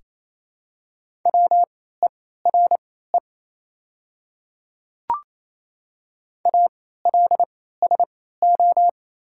Morse code, heard as a keyed tone, about 28 wpm, spaced as for 16 wpm, 700 Hz. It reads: WERE ALSO
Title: 3x Character Spacing - Top 200 Words - 28wpm